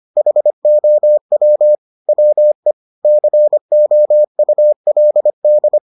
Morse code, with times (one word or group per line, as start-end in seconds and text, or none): 0.2-1.8 HOW
2.1-2.7 WE
3.0-5.8 COULD